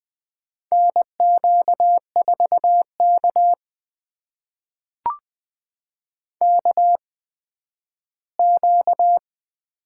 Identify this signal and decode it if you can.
Morse code, heard as a keyed tone, about 20 wpm, spaced as for 12 wpm, 700 Hz. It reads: NQ4K KQ
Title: USA Call Signs - 4 Characters - 8x Character Spacing - 20wpm